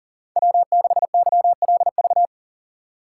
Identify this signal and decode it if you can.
Morse code, heard as a keyed tone, about 40 wpm, 700 Hz. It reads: W6YLV
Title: USA Call Signs - 5 Characters - 4x Character Spacing - 40wpm